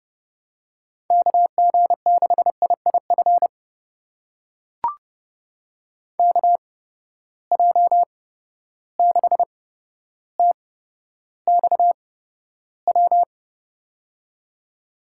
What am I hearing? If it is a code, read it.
Morse code, heard as a keyed tone, about 30 wpm, spaced as for 13 wpm, 700 Hz. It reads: KG6IIF KJ6TXW
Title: USA Call Signs - 6 Characters - 8x Character Spacing - 30wpm